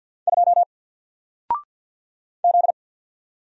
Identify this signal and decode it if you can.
Morse code, heard as a keyed tone, about 50 wpm, 700 Hz. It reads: 2 Z
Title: Single Letter-Number Alt 50wpm